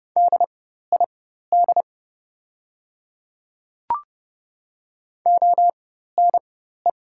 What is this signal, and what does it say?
Morse code, heard as a keyed tone, about 30 wpm, spaced as for 14 wpm, 700 Hz. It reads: DID ONE
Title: 4x Character Spacing - Top 100 Words - 30wpm